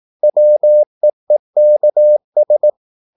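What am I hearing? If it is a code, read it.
Morse code, heard as a keyed tone, about 18 wpm, 600 Hz. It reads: WEEKS
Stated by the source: Morse practice at 18 wpm, English text, from CW med Gnister.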